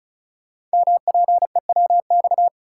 Morse code, heard as a keyed tone, about 36 wpm, 700 Hz. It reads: MPEWX
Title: Code Groups - 5 Letters - 35wpm